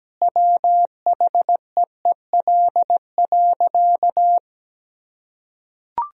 Morse code, heard as a keyed tone, about 17 wpm, 700 Hz. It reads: WHEEL.
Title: Sentences from Top 1000 Words - Review Part 3 - 17wpm